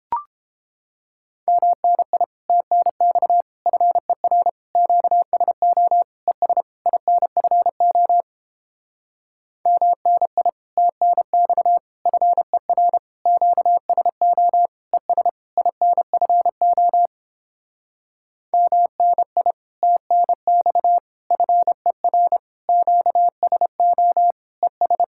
Morse code, heard as a keyed tone, about 31 wpm, 700 Hz. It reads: MNI TNX FER QSO ES INFO MNI TNX FER QSO ES INFO MNI TNX FER QSO ES